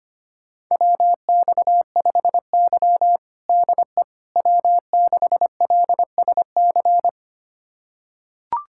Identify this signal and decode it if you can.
Morse code, heard as a keyed tone, about 25 wpm, 700 Hz. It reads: WX5Y DE W6LSC
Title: USA Callsign DE Callsign 25wpm